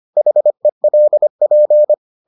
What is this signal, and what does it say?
Morse code, heard as a keyed tone, about 25 wpm, 600 Hz. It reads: HELP